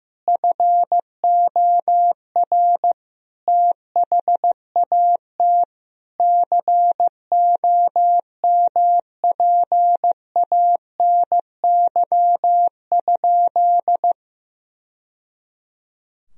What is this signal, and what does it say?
Morse code, heard as a keyed tone, about 15 wpm, 700 Hz. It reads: FOR THAT COMPANY?
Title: Sentences from Top 1000 Words - Review Part 3 - 15wpm